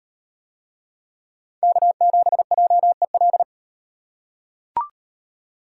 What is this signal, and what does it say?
Morse code, heard as a keyed tone, about 38 wpm, 700 Hz. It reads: K7JEL